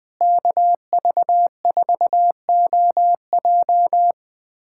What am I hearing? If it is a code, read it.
Morse code, heard as a keyed tone, about 20 wpm, 700 Hz. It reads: KV4OJ
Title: USA Callsign DE Callsign 20wpm